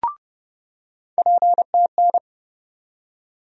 Morse code, heard as a keyed tone, about 31 wpm, 700 Hz. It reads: PTN
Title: USA Call Sign Postfixes - 3 Letters 30wpm